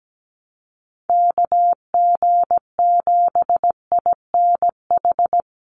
Morse code, heard as a keyed tone, about 17 wpm, 700 Hz. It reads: KG7INH
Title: USA Call Signs 17wpm